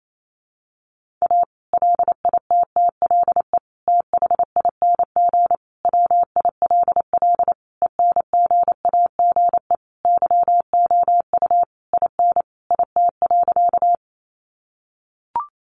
Morse code, heard as a keyed tone, about 28 wpm, 700 Hz. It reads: A LITTLE THING WILL ENGAGE YOU IN IT.
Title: Sentences from Top 1500 Words 28wpm